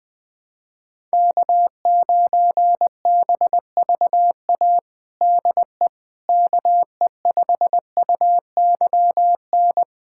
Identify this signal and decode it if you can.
Morse code, heard as a keyed tone, about 20 wpm, 700 Hz. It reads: K9BVA DE KE5UYN
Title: USA Callsign DE Callsign 20wpm